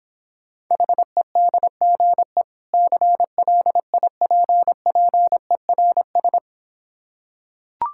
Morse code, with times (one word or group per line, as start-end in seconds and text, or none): 0.7-2.4 HEDGE
2.7-6.4 CLIPPERS